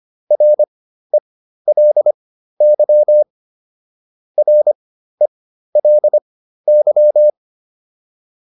Morse code, heard as a keyed tone, about 25 wpm, spaced as for 13 wpm, 600 Hz. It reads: RELY RELY